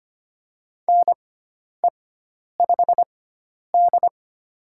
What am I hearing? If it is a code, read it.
Morse code, heard as a keyed tone, about 25 wpm, spaced as for 10 wpm, 700 Hz. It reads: NE5D
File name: Lesson 16g - Letter c - Call Signs 25wpm